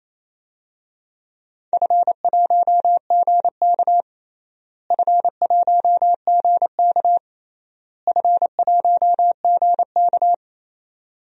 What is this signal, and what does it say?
Morse code, heard as a keyed tone, about 28 wpm, 700 Hz. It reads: F1GK F1GK F1GK
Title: DXCC-100 Call Signs - 4 Characters - Repeated 3x - 28wpm